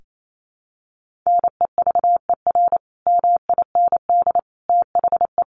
Morse code, heard as a keyed tone, about 28 wpm, 700 Hz. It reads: NEVER MIND THE
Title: Sentences from Top 1400 Words 28wpm